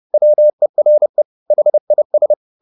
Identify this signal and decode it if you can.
Morse code, heard as a keyed tone, about 30 wpm, 600 Hz. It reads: WERE HIS
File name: the old man and the sea / the old man and the sea 1 30wpm